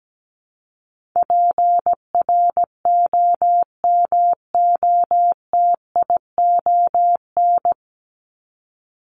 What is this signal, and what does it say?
Morse code, heard as a keyed tone, about 17 wpm, 700 Hz. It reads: PROMOTION
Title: Top 1501-1750 Words 17wpm